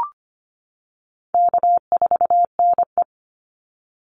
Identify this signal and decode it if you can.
Morse code, heard as a keyed tone, about 25 wpm, 700 Hz. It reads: K4NE
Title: Easy Call Signs - 25wpm